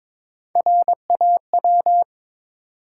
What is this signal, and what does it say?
Morse code, heard as a keyed tone, about 23 wpm, 700 Hz. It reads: RAW
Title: Three Letter Words - 22wpm